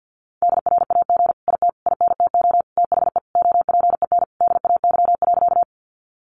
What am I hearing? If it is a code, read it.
Morse code, heard as a keyed tone, about 50 wpm, 700 Hz. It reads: DRAG IT INTO THE OPEN DAY.
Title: Sentences from Top 1500 Words 50wpm